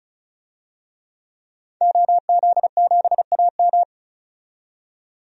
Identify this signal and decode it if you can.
Morse code, heard as a keyed tone, about 36 wpm, 700 Hz. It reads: OZ7AM
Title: DXCC-100 Call Signs - 5 Characters - 35wpm